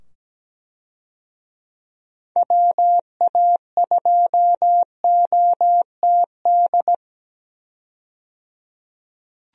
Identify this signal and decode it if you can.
Morse code, heard as a keyed tone, about 17 wpm, 700 Hz. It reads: WA2OTD
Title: USA Call Signs - 6 Characters - 17wpm